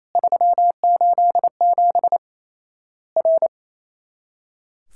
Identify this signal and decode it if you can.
Morse code, heard as a keyed tone, about 28 wpm, 700 Hz. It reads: 387 R
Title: Worked All Europe DX Contest - QTC Exchange - 28wpm